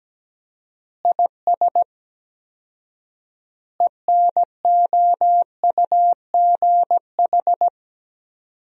Morse code, heard as a keyed tone, about 17 wpm, 700 Hz. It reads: IS ENOUGH
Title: 4x Word-Spacing - Sentences from Top 400 Words - 17wpm